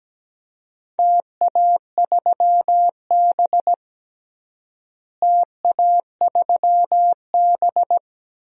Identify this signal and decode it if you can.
Morse code, heard as a keyed tone, about 17 wpm, 700 Hz. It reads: TA3B TA3B
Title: DXCC-100 Call Signs - 4 Characters - Repeated 3x - 17wpm